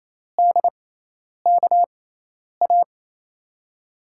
Morse code, heard as a keyed tone, about 28 wpm, spaced as for 10 wpm, 700 Hz. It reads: DKA
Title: USA Call Signs - 6 Characters - 6x Character Spacing - 28wpm